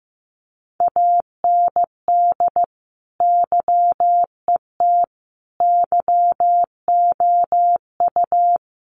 Morse code, heard as a keyed tone, about 15 wpm, 700 Hz. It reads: AND YET YOU
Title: Sentences from Top 1400 Words 15wpm